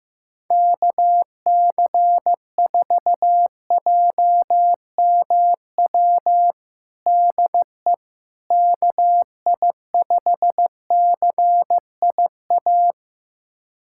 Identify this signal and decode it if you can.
Morse code, heard as a keyed tone, about 15 wpm, 700 Hz. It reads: KC4JMW DE KI5CIA
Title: USA Callsign DE Callsign 15wpm